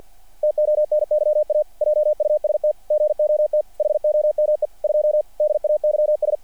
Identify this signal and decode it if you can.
Morse code, heard as a keyed tone, about 48 wpm, 600 Hz. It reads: TONYA JUST GOT HOME 2 DAYS
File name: archived / CRSnet